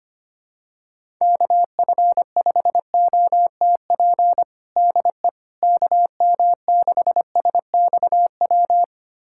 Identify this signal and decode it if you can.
Morse code, heard as a keyed tone, about 25 wpm, 700 Hz. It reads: KF5OTP DE KM6SXW
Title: USA Callsign DE Callsign 25wpm